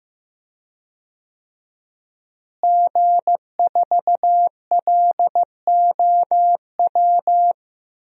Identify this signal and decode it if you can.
Morse code, heard as a keyed tone, about 15 wpm, 700 Hz. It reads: G4LOW
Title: DXCC-100 Call Signs - 5 Characters - Repeated 2x - 15wpm